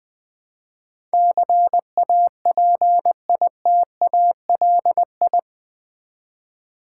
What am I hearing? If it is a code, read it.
Morse code, heard as a keyed tone, about 20 wpm, 700 Hz. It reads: CAPITALI
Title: Build A Word - 4 Letters At A Time - 10-Letter Words - 20wpm